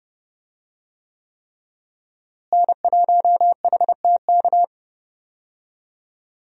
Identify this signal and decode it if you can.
Morse code, heard as a keyed tone, about 31 wpm, 700 Hz. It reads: N1HTK
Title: USA Call Signs - 5 Characters - Repeated 3x - 30wpm